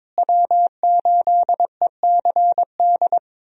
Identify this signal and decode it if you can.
Morse code, heard as a keyed tone, about 22 wpm, 700 Hz. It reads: W8ECD
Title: USA Call Signs - 5 Characters - Repeated 3x - 22wpm